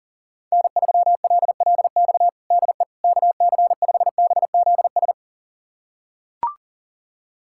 Morse code, heard as a keyed tone, about 37 wpm, 700 Hz. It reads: N3LLX DE KC5BZS
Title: USA Callsign DE Callsign 40wpm